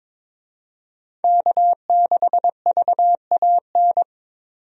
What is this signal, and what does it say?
Morse code, heard as a keyed tone, about 22 wpm, 700 Hz. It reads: K6VAN
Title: USA Call Signs - 5 Characters - 22wpm